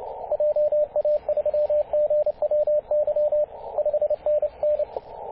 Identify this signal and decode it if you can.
Morse code, heard as a keyed tone, about 28 wpm, 600 Hz, with radio noise. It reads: JA3GWY 5NN
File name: posts / 2016-apr-21-0651z-5w0xg-21-012mhz-cw